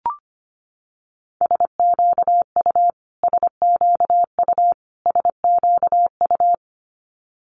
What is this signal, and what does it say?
Morse code, heard as a keyed tone, about 25 wpm, 700 Hz. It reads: SQU SQU SQU
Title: Multi-letter Phonemes Repeated 3x 25wpm